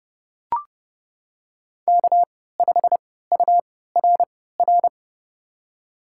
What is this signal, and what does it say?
Morse code, heard as a keyed tone, about 29 wpm, spaced as for 17 wpm, 700 Hz. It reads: K5URR
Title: USA Call Signs - 3x Character Spacing - 30wpm